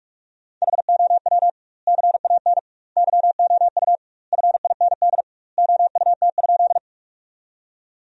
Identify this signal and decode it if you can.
Morse code, heard as a keyed tone, about 46 wpm, 700 Hz. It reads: HOW CAN YOU FIND OUT?
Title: 2x Word-Spacing - Sentences from Top 200 Words - 45wpm